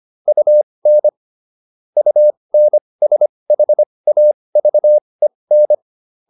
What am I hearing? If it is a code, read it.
Morse code, heard as a keyed tone, about 25 wpm, spaced as for 20 wpm, 600 Hz. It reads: UN UNSHAVEN